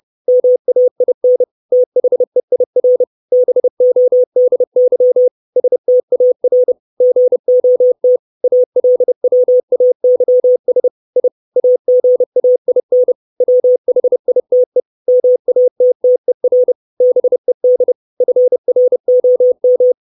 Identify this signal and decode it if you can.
Morse code, heard as a keyed tone, about 30 wpm, 500 Hz. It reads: MAIN THEIR BODY STAR GOT ALWAYS I AGAIN WHITE MATTER BED FROM
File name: 632 Words 30WPM